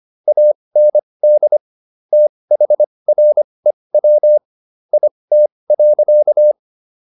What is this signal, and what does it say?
Morse code, heard as a keyed tone, about 25 wpm, spaced as for 20 wpm, 600 Hz. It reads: AND THREW IT.